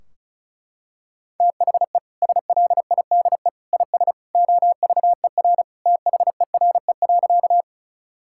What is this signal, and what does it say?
Morse code, heard as a keyed tone, about 34 wpm, 700 Hz. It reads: THE SLIDE IS OVER THERE.
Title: Sentences from Top 1600 Words 35wpm